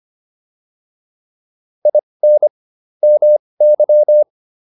Morse code, heard as a keyed tone, about 25 wpm, spaced as for 20 wpm, 600 Hz. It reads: IN MY